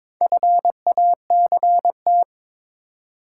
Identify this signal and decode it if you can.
Morse code, heard as a keyed tone, about 22 wpm, 700 Hz. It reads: FACT
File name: Build A Phrase - 2-Word Phrases - 22wpm